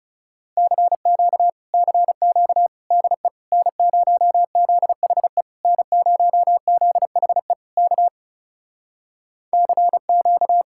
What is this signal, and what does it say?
Morse code, heard as a keyed tone, about 38 wpm, 700 Hz. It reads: CQ CQ DE N0ZHE N0ZHE K CQ